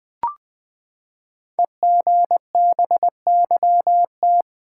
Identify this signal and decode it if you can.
Morse code, heard as a keyed tone, about 21 wpm, 700 Hz. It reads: EGBYT
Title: Code Groups - 5 Letter:Numbers - 20wpm